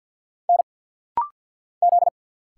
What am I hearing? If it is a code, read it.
Morse code, heard as a keyed tone, about 50 wpm, 700 Hz. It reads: N Z
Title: Single Letter-Number - Rapid-Fire 50wpm